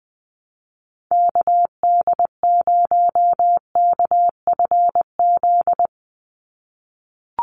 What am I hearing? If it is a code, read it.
Morse code, heard as a keyed tone, about 20 wpm, 700 Hz. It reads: KD0KFZ
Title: USA Call Signs 20wpm